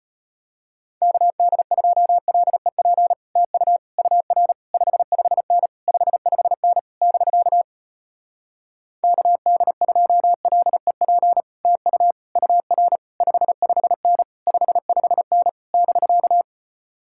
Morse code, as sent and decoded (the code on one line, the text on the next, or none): -.- -.. ..--- .-.. .--. / - ..- / ..- .-. / ..... ..... -. / ..... ..... -. / -...-.- / -.- -.. ..--- .-.. .--. / - ..- / ..- .-. / ..... ..... -. / ..... ..... -. / -...-.-
KD2LEP TU UR 55N 55N <BK> KD2LEP TU UR 55N 55N <BK>